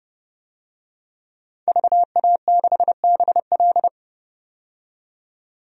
Morse code, heard as a keyed tone, about 30 wpm, 700 Hz. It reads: VA6BL
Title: DXCC-100 Call Signs - 5 Characters - 30wpm